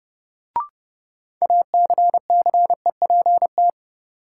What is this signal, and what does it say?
Morse code, heard as a keyed tone, about 31 wpm, 700 Hz. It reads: ACCEPT
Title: Sentences from Top 1300 Words - New Word Drill Alt - 30wpm